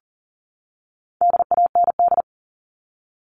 Morse code, heard as a keyed tone, about 40 wpm, 700 Hz. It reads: DAND
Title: All Cut Numbers - 4-Digit 40wpm